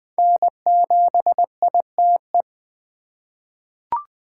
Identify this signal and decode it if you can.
Morse code, heard as a keyed tone, about 20 wpm, 700 Hz. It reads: N7ITE
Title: USA Call Signs - 5 Characters - 4x Character Spacing - 20wpm